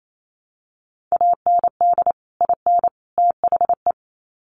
Morse code, heard as a keyed tone, about 28 wpm, 700 Hz. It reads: AND IN THE
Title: Sets of 3 Words 28wpm